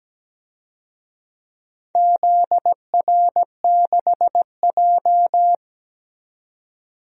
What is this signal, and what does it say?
Morse code, heard as a keyed tone, about 17 wpm, 700 Hz. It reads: ZR6J
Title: DXCC-100 Call Signs - 4 Characters - 17wpm